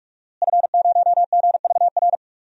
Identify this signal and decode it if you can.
Morse code, heard as a keyed tone, about 47 wpm, 700 Hz. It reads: F0GVR
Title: DXCC-100 Call Signs - 5 Characters - Repeated 3x - 45wpm